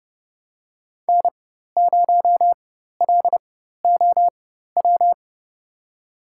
Morse code, heard as a keyed tone, about 30 wpm, spaced as for 14 wpm, 700 Hz. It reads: N0LOW
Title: USA Call Signs - 5 Characters - 4x Character Spacing - 30wpm